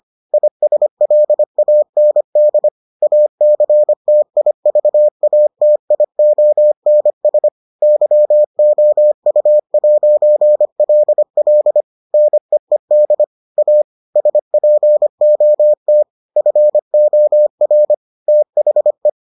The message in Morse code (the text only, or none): ISLAND ACTIVATIONS YOU'LL NEED A SPOT FOR THE